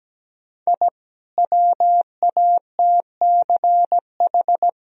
Morse code, heard as a keyed tone, about 17 wpm, 700 Hz. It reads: I WATCH
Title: Sentences from Top 1000 Words - Review Part 1 - 17wpm